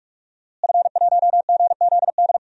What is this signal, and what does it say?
Morse code, heard as a keyed tone, about 47 wpm, 700 Hz. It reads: F1GZD
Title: DXCC-100 Call Signs - 5 Characters - Repeated 2x - 45wpm